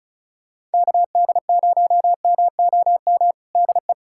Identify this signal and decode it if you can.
Morse code, heard as a keyed tone, about 36 wpm, 700 Hz. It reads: KD0MOM DE
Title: USA Callsign DE Callsign - 6 Characters - 35wpm